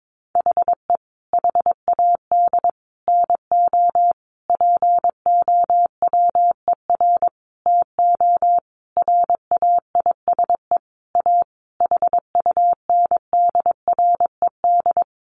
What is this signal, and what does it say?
Morse code, heard as a keyed tone, about 22 wpm, 700 Hz. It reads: HE HAD NO POWER TO RAISE A HUNDRED
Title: Sentences from Top 1400 Words 22wpm